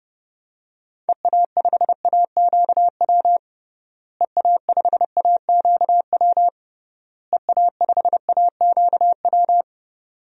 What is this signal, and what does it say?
Morse code, heard as a keyed tone, about 30 wpm, 700 Hz. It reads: EA5AQW EA5AQW EA5AQW
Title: DXCC-100 Call Signs - 6 Characters - Repeated 3x - 30wpm